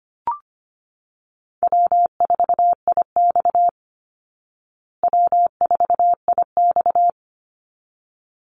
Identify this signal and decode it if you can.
Morse code, heard as a keyed tone, about 25 wpm, 700 Hz. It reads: W4IX W4IX
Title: Easy Call Signs Repeated 2x - 25wpm